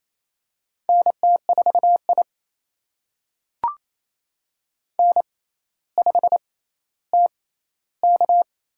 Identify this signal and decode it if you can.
Morse code, heard as a keyed tone, about 28 wpm, spaced as for 14 wpm, 700 Hz. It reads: NT4I N5TK
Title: USA Call Signs - 4 Characters - 6x Character Spacing - 28wpm